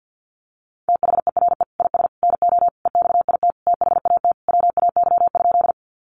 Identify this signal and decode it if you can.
Morse code, heard as a keyed tone, about 51 wpm, 700 Hz. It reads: THERE IS NO EXIT THAT WAY?
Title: Sentences from Top 1100 Words 50wpm